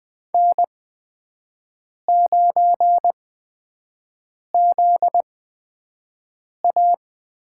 Morse code, heard as a keyed tone, about 20 wpm, spaced as for 5 wpm, 700 Hz. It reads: N9ZA
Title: USA Call Signs - 4 Characters - 8x Character Spacing - 20wpm